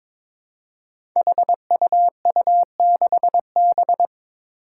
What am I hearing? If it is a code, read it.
Morse code, heard as a keyed tone, about 22 wpm, 700 Hz. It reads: HUU6B
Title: Code Groups - 5 Letter:Numbers - 22wpm